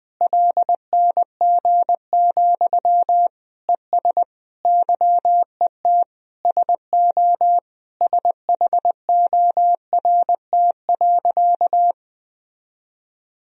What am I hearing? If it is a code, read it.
Morse code, heard as a keyed tone, about 21 wpm, 700 Hz. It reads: LNG, ES YET SO SHORT.
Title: Sentences from Top 500 Words and Abbreviations 20wpm